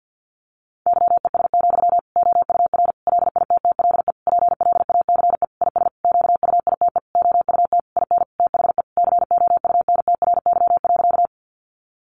Morse code, heard as a keyed tone, about 49 wpm, 700 Hz. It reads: YES, OUR LITTLE PLACE IS QUITE OUT IN THE COUNTRY.
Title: Sentences from Top 700 Words 50wpm